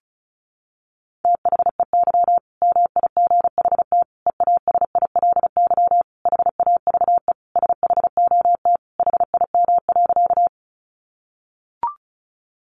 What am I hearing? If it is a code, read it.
Morse code, heard as a keyed tone, about 34 wpm, 700 Hz. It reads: THEY MIGHT EASILY HAVE SHOT HIM.
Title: Sentences from Top 1100 Words 35wpm